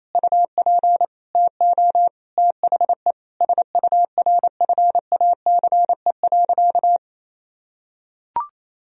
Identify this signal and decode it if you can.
Morse code, heard as a keyed tone, about 28 wpm, 700 Hz. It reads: UP TO THE SURFACE.
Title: Sentences from Top 1000 Words - Review Part 3 - 28wpm